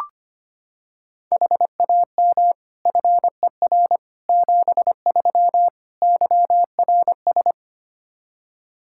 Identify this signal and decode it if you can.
Morse code, heard as a keyed tone, about 25 wpm, 700 Hz. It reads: HAM FER 73 YRS
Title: Practice QSO - Exchange 3 - More Info - 25wpm